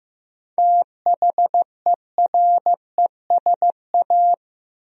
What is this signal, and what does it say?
Morse code, heard as a keyed tone, about 15 wpm, 700 Hz. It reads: THERESA
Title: USA Common Names - 15wpm